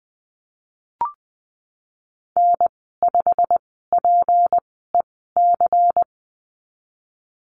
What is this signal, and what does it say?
Morse code, heard as a keyed tone, about 20 wpm, spaced as for 15 wpm, 700 Hz. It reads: N5PEC
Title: 161 Lesson 26g - Letter g - Call Signs 20wpm